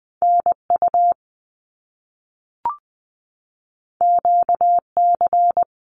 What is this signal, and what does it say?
Morse code, heard as a keyed tone, about 20 wpm, 700 Hz. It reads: NU QC
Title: Canadian Provinces and Territories 20wpm